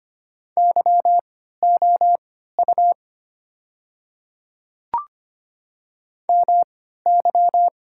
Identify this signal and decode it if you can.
Morse code, heard as a keyed tone, about 25 wpm, spaced as for 14 wpm, 690 Hz. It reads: YOU MY
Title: 3x Character Spacing - Top 300 Words - 25wpm